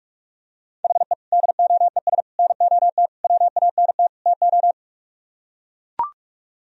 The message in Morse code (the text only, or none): HE DOES NOT WANT TO